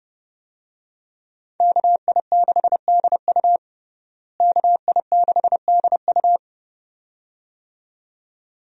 Morse code, heard as a keyed tone, about 30 wpm, 700 Hz. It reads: KI6DU KI6DU
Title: USA Call Signs - 5 Characters - Repeated 2x - 30wpm